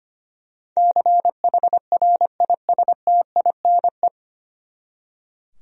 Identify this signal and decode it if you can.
Morse code, heard as a keyed tone, about 25 wpm, 700 Hz. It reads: CHRISTINE
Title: USA Common Names - 25wpm